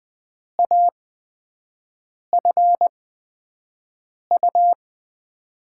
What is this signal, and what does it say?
Morse code, heard as a keyed tone, about 20 wpm, spaced as for 5 wpm, 700 Hz. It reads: AFU
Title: USA Call Signs - 5 Characters - 8x Character Spacing - 20wpm